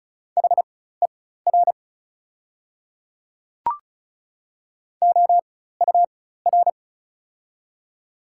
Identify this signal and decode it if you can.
Morse code, heard as a keyed tone, about 36 wpm, spaced as for 16 wpm, 700 Hz. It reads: HER OUR
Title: 4x Character Spacing - Top 200 Words - 35wpm